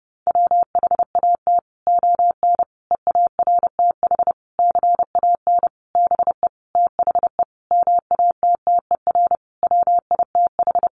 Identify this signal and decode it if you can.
Morse code, heard as a keyed tone, about 30 wpm, 700 Hz. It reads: WHAT ON EARTH CAN BE THE MATTER WITH
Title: Sentences from Top 700 Words 30wpm